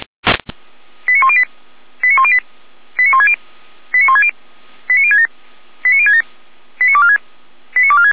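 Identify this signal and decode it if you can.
Five-tone selective call (ZVEI), heard as quick runs of five tones, tones of about 70 ms. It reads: 89108 89108 89160 89160 89067 89067 89246 89246
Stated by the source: ZVEI five-tone selective call 89150 received by SDR, archived by feuerwehr-orlamuende fire brigade server.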